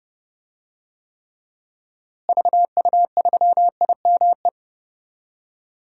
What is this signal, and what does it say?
Morse code, heard as a keyed tone, about 32 wpm, 700 Hz. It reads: VU3IME
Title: DXCC-100 Call Signs - 6 Characters - Repeated 3x - 30wpm